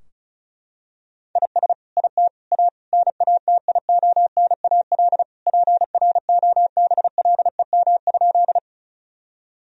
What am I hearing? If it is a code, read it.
Morse code, heard as a keyed tone, about 36 wpm, 700 Hz. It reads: IS IT A NATIONAL PROBLEM?
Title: Sentences from Top 1000 Words - Review Part 4 - 35wpm